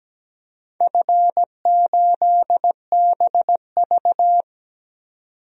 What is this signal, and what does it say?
Morse code, heard as a keyed tone, about 17 wpm, 700 Hz. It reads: F8BV
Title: DXCC-100 Call Signs - 4 Characters - 17wpm